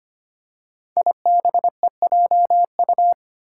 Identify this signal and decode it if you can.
Morse code, heard as a keyed tone, about 25 wpm, 700 Hz. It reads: IBEJU
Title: Five Challenging Letter-Number Combinations 25wpm